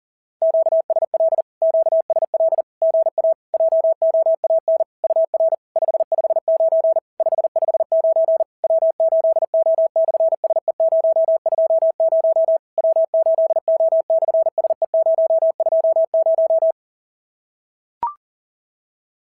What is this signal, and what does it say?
Morse code, heard as a keyed tone, about 40 wpm, 650 Hz. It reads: QSL QSL GA JOAN UR 559 559 W8O/SE020 W8O/SE020
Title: SOTA QSO Practice 40wpm